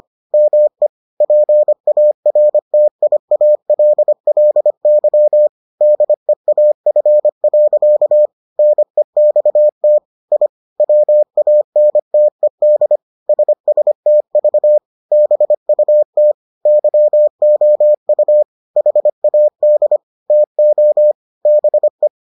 Morse code, heard as a keyed tone, about 25 wpm, 600 Hz. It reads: ME PARTIALLY DEAF. NEXT I WANTED SSTV BUT YOU HAD TO BE